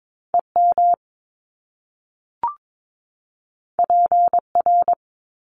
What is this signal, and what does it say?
Morse code, heard as a keyed tone, about 22 wpm, 700 Hz. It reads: EM PR
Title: Top Two Letter Combinations in Words 22wpm